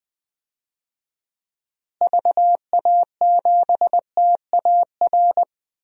Morse code, heard as a keyed tone, about 20 wpm, 700 Hz. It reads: VA7TAR
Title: DXCC-100 Call Signs - 6 Characters - Repeated 3x - 20wpm